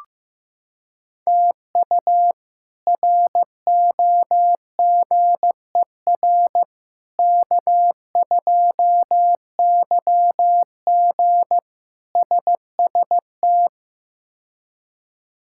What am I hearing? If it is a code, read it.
Morse code, heard as a keyed tone, about 16 wpm, 700 Hz. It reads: TU ROGER K2YG SST
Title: K1USN SST QSO Practice - 15wpm